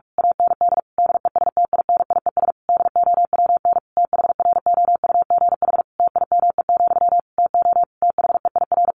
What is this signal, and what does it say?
Morse code, heard as a keyed tone, about 44 wpm, 700 Hz. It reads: AND DESTINIES DOWN THROUGH TIME, TO THEIR
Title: A Sound Of Thunder 45wpm